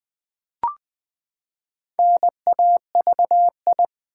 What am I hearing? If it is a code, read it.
Morse code, heard as a keyed tone, about 20 wpm, 700 Hz. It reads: NAVI
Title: Build A Word - 4 Letters At A Time - 10-Letter Words - 20wpm